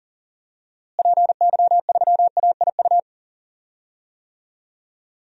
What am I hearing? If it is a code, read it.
Morse code, heard as a keyed tone, about 40 wpm, 700 Hz. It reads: PY3AIU
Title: DXCC-100 Call Signs - 6 Characters - Repeated 3x - 40wpm